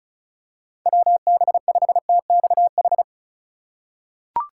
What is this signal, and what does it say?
Morse code, heard as a keyed tone, about 34 wpm, 700 Hz. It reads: WB5TXH
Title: USA Call Signs - 6 Characters - 4x Character Spacing - 35wpm